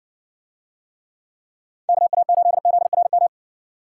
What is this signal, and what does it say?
Morse code, heard as a keyed tone, about 60 wpm, 700 Hz. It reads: KA8ZRM